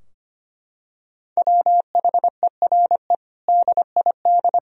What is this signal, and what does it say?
Morse code, heard as a keyed tone, about 25 wpm, 700 Hz. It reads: WHERE DID
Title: Sentences from Top 1000 Words - Review Part 3 - 25wpm